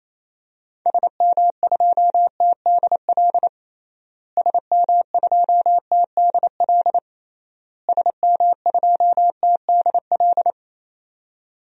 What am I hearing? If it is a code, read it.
Morse code, heard as a keyed tone, about 28 wpm, 700 Hz. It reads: SM2TDL SM2TDL SM2TDL
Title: DXCC-100 Call Signs - 6 Characters - Repeated 3x - 28wpm